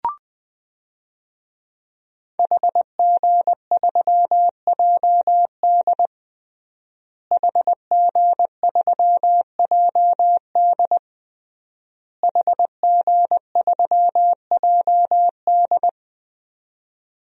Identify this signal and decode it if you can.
Morse code, heard as a keyed tone, about 20 wpm, 700 Hz. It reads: HG3JD HG3JD HG3JD
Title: DXCC-100 Call Signs - 5 Characters - Repeated 3x - 20wpm